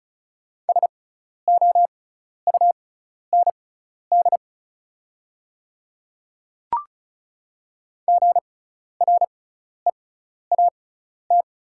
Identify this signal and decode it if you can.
Morse code, heard as a keyed tone, about 34 wpm, spaced as for 12 wpm, 690 Hz. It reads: SOUND GREAT
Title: 6x Character Spacing - Top 500 Words - 35wpm